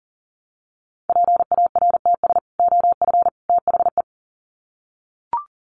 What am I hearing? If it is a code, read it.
Morse code, heard as a keyed tone, about 40 wpm, 700 Hz. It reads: PARTS OF THE
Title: Sets of 3 Words 40wpm